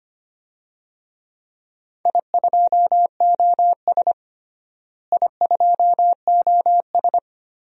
Morse code, heard as a keyed tone, about 25 wpm, 700 Hz. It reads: I2OS I2OS
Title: DXCC-100 Call Signs - 4 Characters - Repeated 3x - 25wpm